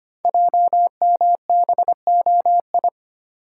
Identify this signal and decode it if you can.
Morse code, heard as a keyed tone, about 25 wpm, 700 Hz. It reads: JMBOI
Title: Code Groups - 5 Letters - 25wpm